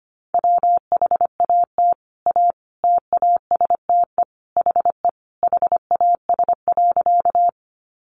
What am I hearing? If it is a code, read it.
Morse code, heard as a keyed tone, about 25 wpm, 700 Hz. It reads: WHAT A TASTE HE HAS.
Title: Sentences from Top 1100 Words 25wpm